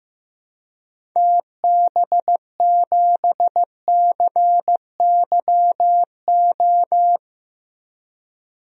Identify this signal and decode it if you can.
Morse code, heard as a keyed tone, about 15 wpm, 700 Hz. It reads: TB7CYO
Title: DXCC-100 Call Signs - 6 Characters - Repeated 3x - 15wpm